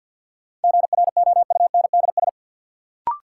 Morse code, heard as a keyed tone, about 53 wpm, 700 Hz. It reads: GROUNDS
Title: Sentences from Top 1300 Words - New Word Drill - 50wpm